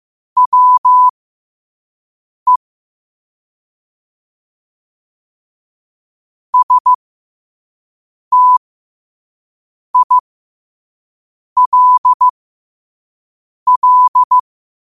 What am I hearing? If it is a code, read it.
Morse code, heard as a keyed tone, about 15 wpm, spaced as for 5 wpm, 1000 Hz. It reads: WE STILL
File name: 5wpm